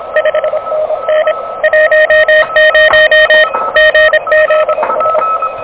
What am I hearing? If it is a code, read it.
Morse code, heard as a keyed tone, about 26 wpm, 590 Hz, with radio noise. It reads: 5NN 10 GZS
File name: OK QRP 2023 / OK QRP 2023